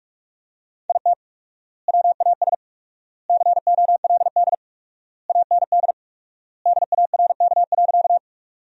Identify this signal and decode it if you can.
Morse code, heard as a keyed tone, about 46 wpm, 700 Hz. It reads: IT WAS COLD AND DARK.
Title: 4x Word-Spacing - Sentences from Top 500 Words - 45wpm